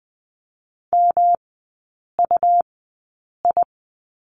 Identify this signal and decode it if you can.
Morse code, heard as a keyed tone, about 20 wpm, spaced as for 8 wpm, 700 Hz. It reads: MUI
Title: 184 Lesson 30e - Letter b - Three Characters 20wpm